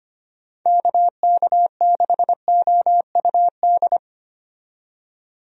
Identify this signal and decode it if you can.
Morse code, heard as a keyed tone, about 25 wpm, 700 Hz. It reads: KK6OUD
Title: USA Call Signs - 6 Characters - 25wpm